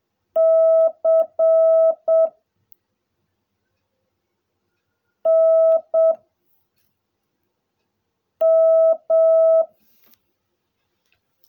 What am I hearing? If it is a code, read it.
Morse code, heard as a keyed tone, about 7 wpm, 660 Hz. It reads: C N M
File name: jh8cbh / 2021cw / cw16a